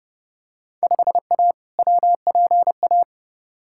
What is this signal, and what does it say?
Morse code, heard as a keyed tone, about 30 wpm, 700 Hz. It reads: 5A WPA
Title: Field Day Exchanges Encore - 30wpm